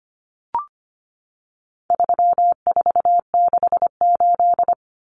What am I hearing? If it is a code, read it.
Morse code, heard as a keyed tone, about 25 wpm, 700 Hz. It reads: 3468
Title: Numbers - 4-Digit 25wpm